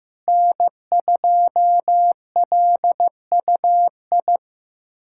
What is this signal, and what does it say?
Morse code, heard as a keyed tone, about 15 wpm, 700 Hz. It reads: N2LUI
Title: USA Callsign DE Callsign - 5 Characters - 15wpm